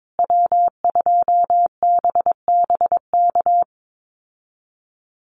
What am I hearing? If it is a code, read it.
Morse code, heard as a keyed tone, about 22 wpm, 700 Hz. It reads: W2BBK
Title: USA Call Signs Repeated 2x 22wpm